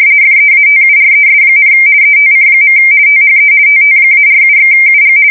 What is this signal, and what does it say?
RTTY, about 50 baud. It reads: FEATURES, ETC. THE TT-47/UG
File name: fsk170-50B / teletype-corp